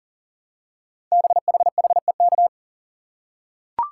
Morse code, heard as a keyed tone, about 41 wpm, 700 Hz. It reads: BHHEK
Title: Code Groups - 5 Letters - 40wpm